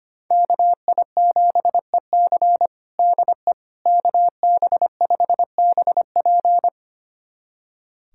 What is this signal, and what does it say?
Morse code, heard as a keyed tone, about 25 wpm, 700 Hz. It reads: KI7EC DE KB5BP
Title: USA Callsign DE Callsign - 5 Characters - 25wpm